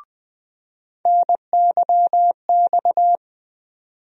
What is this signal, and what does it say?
Morse code, heard as a keyed tone, about 20 wpm, 700 Hz. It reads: NYX
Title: USA Call Sign Postfixes - 3 Letters 20wpm